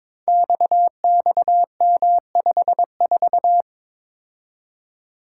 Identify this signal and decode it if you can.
Morse code, heard as a keyed tone, about 22 wpm, 700 Hz. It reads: XXM54
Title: Code Groups - 5 Letter:Numbers - 22wpm